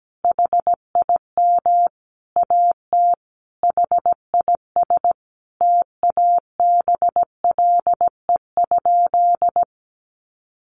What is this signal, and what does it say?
Morse code, heard as a keyed tone, about 17 wpm, 700 Hz. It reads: HIM AT HIS TABLE?
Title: Sentences from Top 1500 Words 17wpm